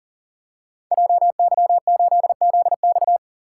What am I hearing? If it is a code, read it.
Morse code, heard as a keyed tone, about 39 wpm, 700 Hz. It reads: JY8ZX
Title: Code Groups - 5 Letter:Numbers - 40wpm